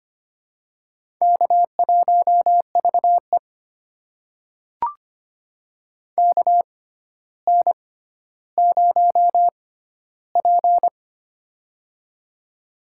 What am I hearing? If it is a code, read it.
Morse code, heard as a keyed tone, about 25 wpm, spaced as for 13 wpm, 700 Hz. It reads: K1VE KN0P
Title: USA Call Signs - 6x Character Spacing - 25wpm